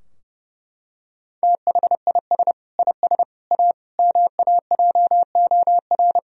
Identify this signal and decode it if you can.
Morse code, heard as a keyed tone, about 30 wpm, 700 Hz. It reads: THIS IS A MAJOR
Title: Sentences from Top 1000 Words - Review Part 4 - 30wpm